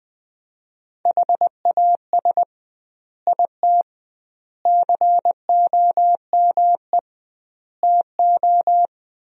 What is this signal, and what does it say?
Morse code, heard as a keyed tone, about 20 wpm, 700 Hz. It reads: HAS IT COME TO
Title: 2x Word-Spacing - Sentences from Top 100 Words - 20wpm